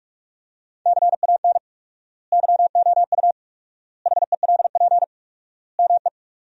Morse code, heard as a keyed tone, about 46 wpm, 700 Hz. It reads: CAN YOU HELP ME
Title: 4x Word-Spacing - Sentences from Top 200 Words - 45wpm